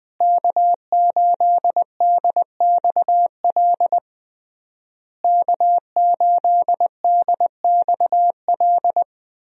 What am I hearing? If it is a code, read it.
Morse code, heard as a keyed tone, about 20 wpm, 700 Hz. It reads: K8DXL K8DXL
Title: USA Call Signs - 5 Characters - Repeated 3x - 20wpm